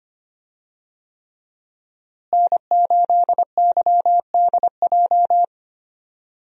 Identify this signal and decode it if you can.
Morse code, heard as a keyed tone, about 25 wpm, 700 Hz. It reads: N8YDJ
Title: USA Call Signs - 5 Characters - Repeated 3x - 25wpm